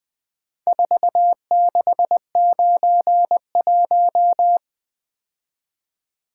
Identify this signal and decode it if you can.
Morse code, heard as a keyed tone, about 20 wpm, 700 Hz. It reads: 4691
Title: Four Letter-Number Combinations - 20wpm